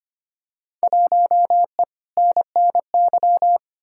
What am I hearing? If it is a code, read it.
Morse code, heard as a keyed tone, about 25 wpm, 700 Hz. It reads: 1E NNY
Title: Field Day Exchanges - 25wpm